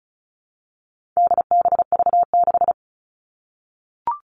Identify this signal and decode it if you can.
Morse code, heard as a keyed tone, about 34 wpm, 700 Hz. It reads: DBV6
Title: All Cut Numbers - 4-Digit 35wpm